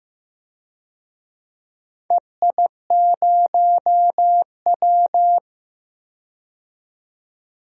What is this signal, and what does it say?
Morse code, heard as a keyed tone, about 15 wpm, 700 Hz. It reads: EI0W
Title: DXCC-100 Call Signs - 4 Characters - 15wpm